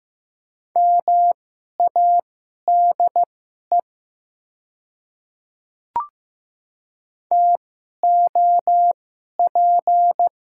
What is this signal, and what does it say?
Morse code, heard as a keyed tone, about 15 wpm, spaced as for 11 wpm, 700 Hz. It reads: MADE TOP
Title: 2x Character Spacing - Top 500 Words - 15wpm